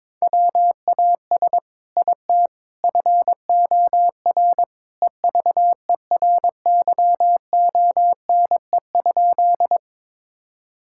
Morse code, heard as a keyed tone, about 22 wpm, 700 Hz. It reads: WAS IT FOR EVERYONE?
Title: Sentences from Top 1000 Words - Review Part 5 - 22wpm